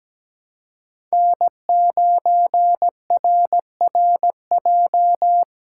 Morse code, heard as a keyed tone, about 17 wpm, 700 Hz. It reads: N9RRJ
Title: USA Call Signs - 5 Characters - 6x Character Spacing - 17wpm